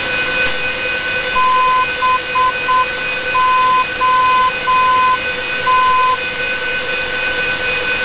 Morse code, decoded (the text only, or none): BOT